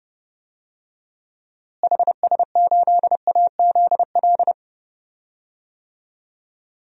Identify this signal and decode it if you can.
Morse code, heard as a keyed tone, about 32 wpm, 700 Hz. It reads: HS8AZL
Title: DXCC-100 Call Signs - 6 Characters - Repeated 2x - 30wpm